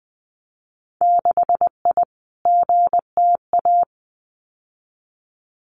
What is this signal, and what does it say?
Morse code, heard as a keyed tone, about 20 wpm, 700 Hz. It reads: 6I GTA
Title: Winter Field Day Exchanges Encore 20wpm